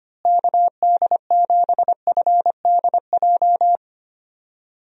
Morse code, heard as a keyed tone, about 25 wpm, 700 Hz. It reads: KD7FDJ
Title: USA Call Signs - 6 Characters - Repeated 3x - 25wpm